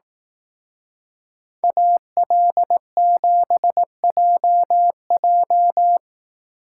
Morse code, heard as a keyed tone, about 18 wpm, 700 Hz. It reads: AL7JJ
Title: Speed Racing - USA 2x2 Call Signs 15wpm